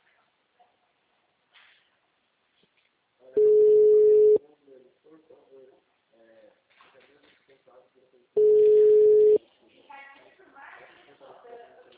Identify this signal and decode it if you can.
Ringback tone (Europe, 425 Hz), with speech.
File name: out- -2005- - - .14560